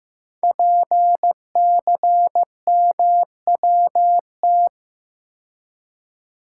Morse code, heard as a keyed tone, about 15 wpm, 700 Hz. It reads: PCMWT
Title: Code Groups - 5 Letters - 15wpm